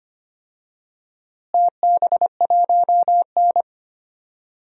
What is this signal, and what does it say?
Morse code, heard as a keyed tone, about 25 wpm, 700 Hz. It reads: TB1N
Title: DXCC-100 Call Signs - 4 Characters - Repeated 3x - 25wpm